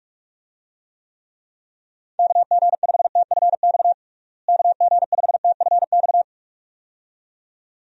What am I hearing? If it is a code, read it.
Morse code, heard as a keyed tone, about 46 wpm, 700 Hz. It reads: KG5TFX KG5TFX
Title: USA Call Signs - 6 Characters - Repeated 2x - 45wpm